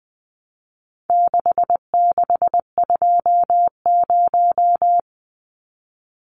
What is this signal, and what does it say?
Morse code, heard as a keyed tone, about 20 wpm, 700 Hz. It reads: 6620
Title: Numbers - 4-Digit 20wpm